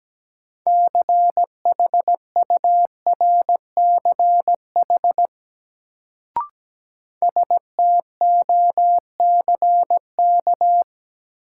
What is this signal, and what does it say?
Morse code, heard as a keyed tone, about 17 wpm, 700 Hz. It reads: CHURCH STOCK
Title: Sets of 5 Words - New Word Drill - 17wpm